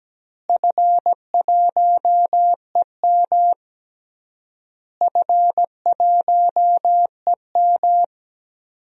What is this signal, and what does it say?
Morse code, heard as a keyed tone, about 17 wpm, 700 Hz. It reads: F1EM F1EM
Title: DXCC-100 Call Signs - 4 Characters - Repeated 3x - 17wpm